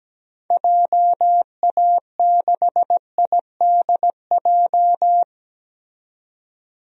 Morse code, heard as a keyed tone, about 17 wpm, 700 Hz. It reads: JA6IDJ
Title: DXCC-100 Call Signs - 6 Characters - 17wpm